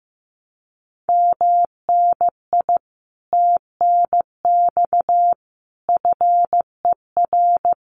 Morse code, heard as a keyed tone, about 16 wpm, 700 Hz. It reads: MNI TNX FER
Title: Common QSO Phrases Encore 15wpm